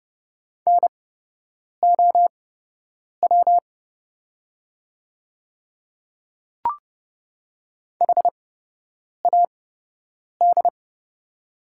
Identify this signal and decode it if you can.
Morse code, heard as a keyed tone, about 29 wpm, spaced as for 8 wpm, 700 Hz. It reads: NOW HAD
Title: 8x Character Spacing - Top 100 Words - 30wpm